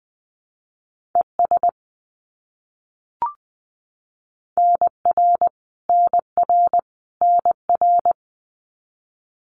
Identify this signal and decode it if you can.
Morse code, heard as a keyed tone, about 20 wpm, 700 Hz. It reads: ES NR NR NR
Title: Abbreviations Repeated 3x - 20wpm